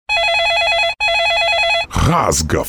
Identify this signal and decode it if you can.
There is no code.